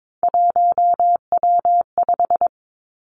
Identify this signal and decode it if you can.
Morse code, heard as a keyed tone, about 22 wpm, 700 Hz. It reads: 1W5
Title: Three Challenging Letter-Number Combinations 22wpm